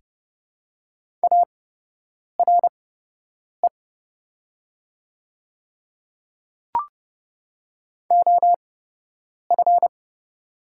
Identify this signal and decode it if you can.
Morse code, heard as a keyed tone, about 30 wpm, spaced as for 8 wpm, 700 Hz. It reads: ARE OF
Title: 8x Character Spacing - Top 100 Words - 30wpm